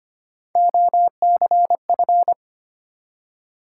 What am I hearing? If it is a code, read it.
Morse code, heard as a keyed tone, about 31 wpm, 700 Hz. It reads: OCF